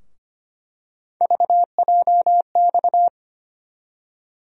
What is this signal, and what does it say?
Morse code, heard as a keyed tone, about 25 wpm, 700 Hz. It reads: VJX
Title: USA Call Sign Postfixes - 3 Letters 25wpm